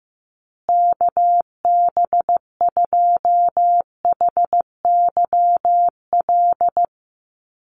Morse code, heard as a keyed tone, about 15 wpm, 700 Hz. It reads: KB2HYL
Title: USA Call Signs Repeated 2x 15wpm